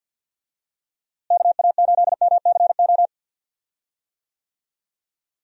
Morse code, heard as a keyed tone, about 50 wpm, 700 Hz. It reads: KA8MCO
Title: USA Call Signs - 6 Characters - Repeated 3x - 50wpm